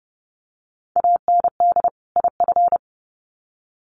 Morse code, heard as a keyed tone, about 30 wpm, 700 Hz. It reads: AND IF
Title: Sets of 2 Words 30wpm